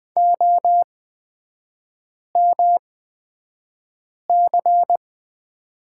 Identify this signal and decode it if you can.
Morse code, heard as a keyed tone, about 20 wpm, spaced as for 5 wpm, 700 Hz. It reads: OMC
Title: 243 Lesson 39e - Letter z - Three Characters 20wpm